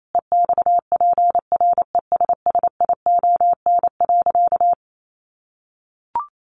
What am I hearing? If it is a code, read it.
Morse code, heard as a keyed tone, about 28 wpm, 700 Hz. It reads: EXPRESSION.
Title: Sentences from Top 1200 Words 28wpm